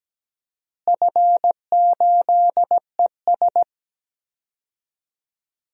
Morse code, heard as a keyed tone, about 17 wpm, 700 Hz. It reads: F8ES
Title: DXCC-100 Call Signs - 4 Characters - Repeated 2x - 17wpm